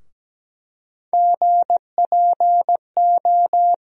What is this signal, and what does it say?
Morse code, heard as a keyed tone, about 17 wpm, 700 Hz. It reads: GPO
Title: USA Call Sign Postfixes - 3 Letters 17wpm